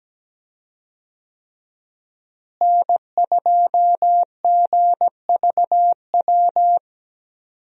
Morse code, heard as a keyed tone, about 17 wpm, 700 Hz. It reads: N2GVW
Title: USA Call Signs - 5 Characters - Repeated 3x - 17wpm